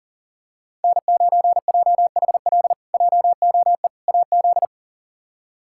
Morse code, heard as a keyed tone, about 40 wpm, 700 Hz. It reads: N9JHL JOE AZ
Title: K1USN SST QSO Practice - 40wpm